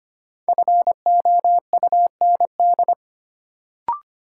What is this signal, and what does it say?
Morse code, heard as a keyed tone, about 25 wpm, 700 Hz. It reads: FOUND
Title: Sentences from Top 300 Words - New Word Drill - 25wpm